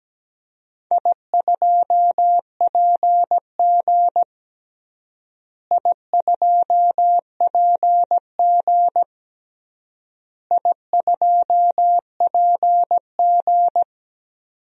Morse code, heard as a keyed tone, about 17 wpm, 700 Hz. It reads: I2PG I2PG I2PG
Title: DXCC-100 Call Signs - 4 Characters - Repeated 3x - 17wpm